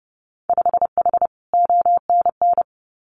Morse code, heard as a keyed tone, about 30 wpm, 700 Hz. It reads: 5H ONN
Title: Winter Field Day Exchanges Encore 30wpm